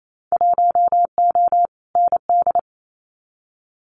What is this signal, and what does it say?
Morse code, heard as a keyed tone, about 28 wpm, 700 Hz. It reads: 1O ND
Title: Winter Field Day Exchanges 28wpm